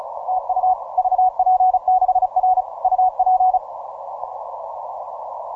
Morse code, heard as a keyed tone, about 33 wpm, 740 Hz. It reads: TU VP6R UP
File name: simple-file-list / VP6R 17MCW